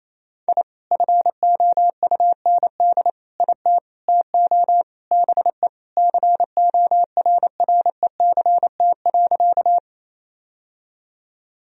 Morse code, heard as a keyed tone, about 28 wpm, 700 Hz. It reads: I FOUND IT TO BE CORRECT.
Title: Sentences from Top 1000 Words - Review Part 2 - 28wpm